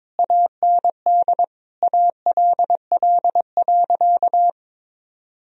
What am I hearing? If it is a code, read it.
Morse code, heard as a keyed tone, about 22 wpm, 700 Hz. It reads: AND ALL.
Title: Sentences from Top 1000 Words - Review Part 2 - 22wpm